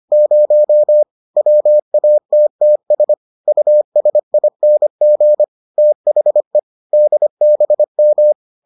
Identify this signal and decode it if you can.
Morse code, heard as a keyed tone, about 25 wpm, 600 Hz. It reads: 0 WATTS USING THE DBM